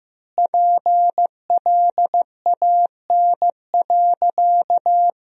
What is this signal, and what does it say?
Morse code, heard as a keyed tone, about 15 wpm, 700 Hz. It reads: PLAN.
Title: 3x Word-Spacing - Sentences from Top 500 Words - 15wpm